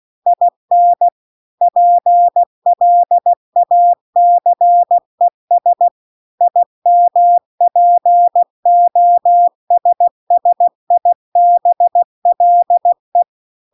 Morse code, heard as a keyed tone, about 16 wpm, 700 Hz. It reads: IN PLACES IMPOSSIBLE